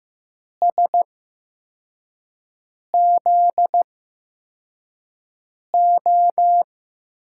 Morse code, heard as a keyed tone, about 15 wpm, spaced as for 4 wpm, 700 Hz. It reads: SZO
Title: USA Call Signs - 5 Characters - 8x Character Spacing - 15wpm